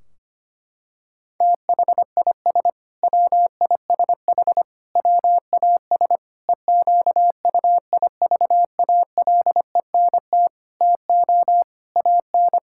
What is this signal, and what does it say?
Morse code, heard as a keyed tone, about 25 wpm, 700 Hz. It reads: THIS WISH WAS EQUIVALENT TO AN
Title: Sentences from Top 1600 Words 25wpm